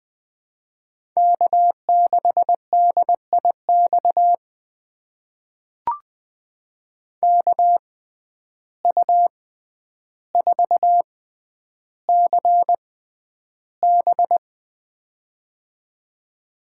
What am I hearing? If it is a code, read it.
Morse code, heard as a keyed tone, about 20 wpm, spaced as for 10 wpm, 700 Hz. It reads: K6DIX KU4CB
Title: USA Call Signs - 5 Characters - 6x Character Spacing - 20wpm